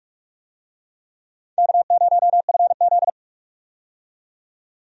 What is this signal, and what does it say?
Morse code, heard as a keyed tone, about 45 wpm, 700 Hz. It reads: K0FZ
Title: USA Call Signs - 4 Characters - 45wpm